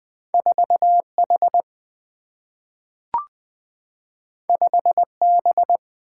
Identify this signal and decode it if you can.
Morse code, heard as a keyed tone, about 20 wpm, 700 Hz. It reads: 4 H 5 B
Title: Single-Letter Single-Number 20wpm